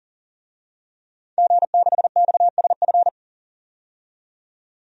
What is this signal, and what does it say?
Morse code, heard as a keyed tone, about 40 wpm, 700 Hz. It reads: G6XSF
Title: DXCC-100 Call Signs - 5 Characters - 40wpm